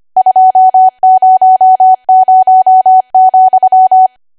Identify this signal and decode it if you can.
Morse code, heard as a keyed tone, about 25 wpm, 750 Hz. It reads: 200,
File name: Archive / 25 WPM / 25WPM